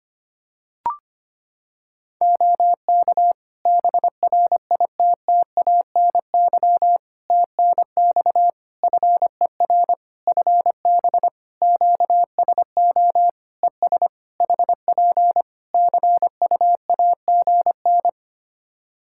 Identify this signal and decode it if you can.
Morse code, heard as a keyed tone, about 25 wpm, 700 Hz. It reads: OK BRITTANY TNX FER FB QSO ES HP CUAGN
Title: Practice QSO - Exchange 4 - Ending - 25wpm